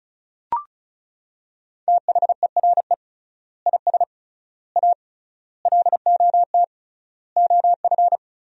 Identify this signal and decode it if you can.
Morse code, heard as a keyed tone, about 36 wpm, 700 Hz. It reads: THERE IS A LOT OF
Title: 3x Word-Spacing - Sentences from Top 500 Words - 35wpm